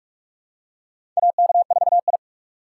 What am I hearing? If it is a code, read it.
Morse code, heard as a keyed tone, about 46 wpm, 700 Hz. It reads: AK4I
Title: USA Call Signs - 4 Characters - 2x Character Spacing - 45wpm